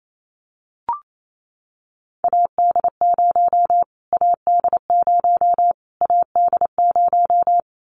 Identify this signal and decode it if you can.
Morse code, heard as a keyed tone, about 28 wpm, 700 Hz. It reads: AD0 AD0 AD0
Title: Callsign Prefixes for Top 25 DXCC - Repeated 3x - 28wpm